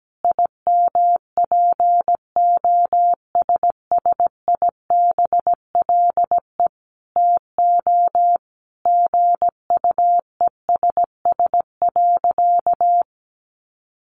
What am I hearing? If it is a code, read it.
Morse code, heard as a keyed tone, about 17 wpm, 700 Hz. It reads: IMPOSSIBLE TO GUESS.
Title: Sentences from Top 1500 Words 17wpm